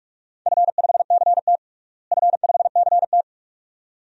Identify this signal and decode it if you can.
Morse code, heard as a keyed tone, about 43 wpm, 700 Hz. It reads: F5CT F5CT
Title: DXCC-100 Call Signs - 4 Characters - Repeated 2x - 45wpm